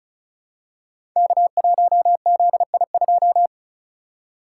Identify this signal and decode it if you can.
Morse code, heard as a keyed tone, about 36 wpm, 690 Hz. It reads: K1ZI2
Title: Code Groups - 5 Letter:Numbers - 35wpm